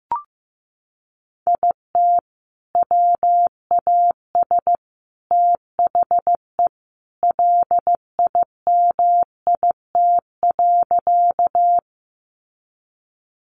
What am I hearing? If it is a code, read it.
Morse code, heard as a keyed tone, about 15 wpm, 700 Hz. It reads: IT WAS THE LIMIT.
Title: Sentences from Top 1500 Words 15wpm